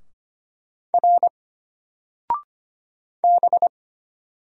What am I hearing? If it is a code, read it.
Morse code, heard as a keyed tone, about 25 wpm, 700 Hz. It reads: R B
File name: Single-Letter-Number-25wpm